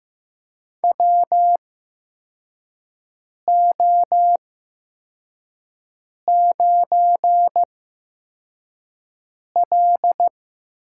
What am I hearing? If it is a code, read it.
Morse code, heard as a keyed tone, about 15 wpm, spaced as for 4 wpm, 700 Hz. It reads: WO9L
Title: USA Call Signs - 4 Characters - 8x Character Spacing - 15wpm